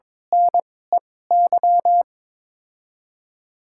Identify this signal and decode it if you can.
Morse code, heard as a keyed tone, about 22 wpm, spaced as for 16 wpm, 700 Hz. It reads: NEY
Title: 2x Character Spacing - Top 500 Words - 22wpm